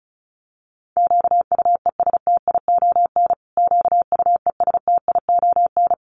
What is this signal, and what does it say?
Morse code, heard as a keyed tone, about 34 wpm, 700 Hz. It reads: QUESTION QUESTION
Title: Top 500 Words Repeated 35wpm